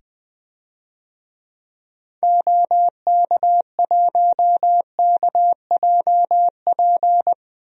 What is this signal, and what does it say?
Morse code, heard as a keyed tone, about 20 wpm, 700 Hz. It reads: OK1KJP
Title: DXCC-100 Call Signs - 6 Characters - 20wpm